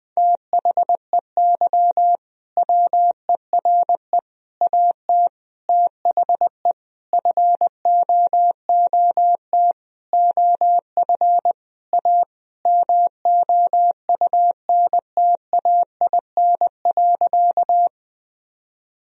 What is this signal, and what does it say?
Morse code, heard as a keyed tone, about 20 wpm, 700 Hz. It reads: THEY WERE AT THE FOOT OF A MOUNTAIN.
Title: Sentences from Top 1000 Words - Review Part 3 - 20wpm